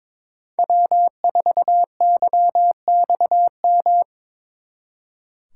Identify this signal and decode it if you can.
Morse code, heard as a keyed tone, about 26 wpm, 700 Hz. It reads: W4YXM
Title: Speed Racing - USA 1x3 Call Signs 22wpm